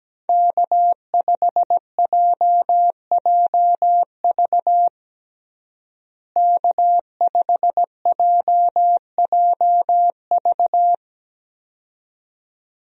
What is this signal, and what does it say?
Morse code, heard as a keyed tone, about 17 wpm, 700 Hz. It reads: K5JJV K5JJV
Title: USA Call Signs - 5 Characters - Repeated 2x - 17wpm